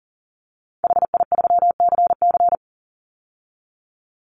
Morse code, heard as a keyed tone, about 40 wpm, 700 Hz. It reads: HI3CC
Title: Call Signs - 40wpm